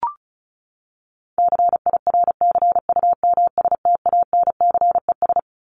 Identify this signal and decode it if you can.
Morse code, heard as a keyed tone, about 34 wpm, 700 Hz. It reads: CIRCUMSTANCES
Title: ES:S Suffix - Top 100 Words with Base Words 35wpm